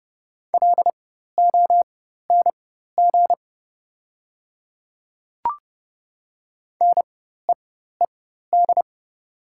Morse code, heard as a keyed tone, about 30 wpm, spaced as for 14 wpm, 700 Hz. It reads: LONG NEED
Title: 4x Character Spacing - Top 500 Words - 30wpm